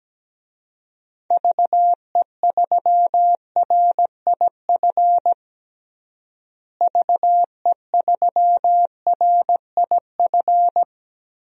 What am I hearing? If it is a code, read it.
Morse code, heard as a keyed tone, about 17 wpm, 700 Hz. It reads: VE3RIF VE3RIF
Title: DXCC-100 Call Signs - 6 Characters - Repeated 3x - 17wpm